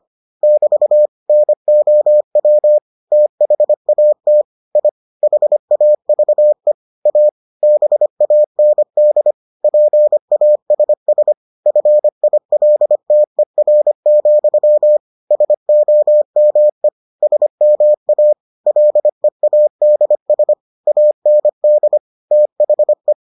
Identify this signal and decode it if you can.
Morse code, heard as a keyed tone, about 25 wpm, 600 Hz. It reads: <BT> NOW THAT I HAVE A BAND PASS FILTER, SOME SMA LEADS AND THE